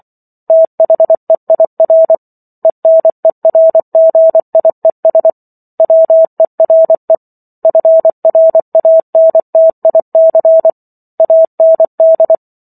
Morse code, heard as a keyed tone, about 24 wpm, 650 Hz. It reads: THEIR ENERGIES WERE FRANTIC AND
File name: Chapter 15 - 24 WPM